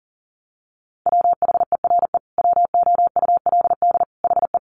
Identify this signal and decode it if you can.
Morse code, heard as a keyed tone, about 40 wpm, 700 Hz. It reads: WHERE WOULD HE
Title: Sentences from Top 1300 Words 40wpm